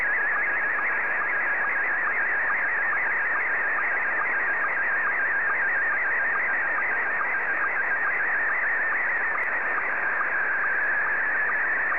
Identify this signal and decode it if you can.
SSTV (slow-scan television), Scottie 1.